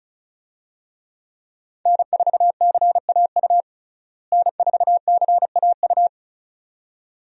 Morse code, heard as a keyed tone, about 34 wpm, 700 Hz. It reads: N4CAU N4CAU
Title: USA Call Signs - 5 Characters - Repeated 2x - 35wpm